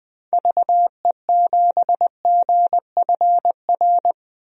Morse code, heard as a keyed tone, about 20 wpm, 700 Hz. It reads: VE7GFR
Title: DXCC-100 Call Signs - 6 Characters - Repeated 3x - 20wpm